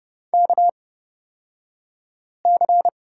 Morse code, heard as a keyed tone, about 31 wpm, spaced as for 5 wpm, 700 Hz. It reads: KC
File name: Lesson 38d - Letter q - Two Characters 30wpm